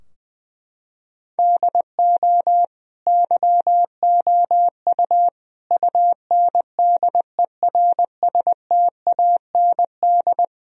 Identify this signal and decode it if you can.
Morse code, heard as a keyed tone, about 21 wpm, 700 Hz. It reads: DO YOU UNDERSTAND
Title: Sentences from Top 1000 Words - Review Part 4 - 20wpm